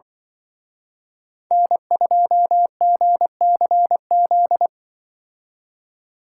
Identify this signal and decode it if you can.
Morse code, heard as a keyed tone, about 23 wpm, 700 Hz. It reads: N2GCZ
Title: Speed Racing - USA 1x3 Call Signs 20wpm